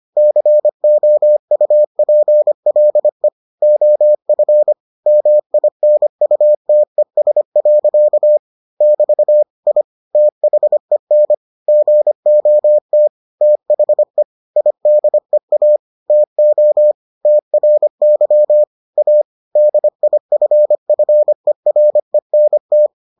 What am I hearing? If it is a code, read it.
Morse code, heard as a keyed tone, about 25 wpm, 600 Hz. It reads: COUPLE OF MINUTES. <BT> I THEN GOT THE IDEA TO TRY A DIFFERENT